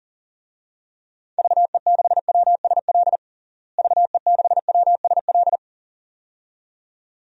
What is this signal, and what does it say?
Morse code, heard as a keyed tone, about 40 wpm, 700 Hz. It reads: VE6WSL VE6WSL
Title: DXCC-100 Call Signs - 6 Characters - Repeated 2x - 40wpm